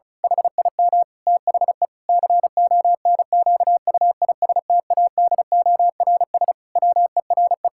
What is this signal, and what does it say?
Morse code, heard as a keyed tone, about 34 wpm, 700 Hz. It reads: HIM THE CONQUISTADORS WERE
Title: Pangram Challenge v2 - 35wpm